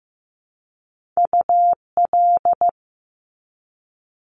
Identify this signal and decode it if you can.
Morse code, heard as a keyed tone, about 15 wpm, 700 Hz. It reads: UL
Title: Top Two Letter Combinations in Words 15wpm